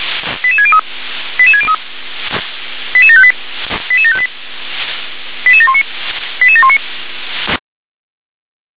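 Five-tone selective call (ZVEI), heard as quick runs of five tones, tones of about 70 ms. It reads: 88603 88603 88758 88758 88719 88719
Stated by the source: ZVEI five-tone selective call received on an SDR.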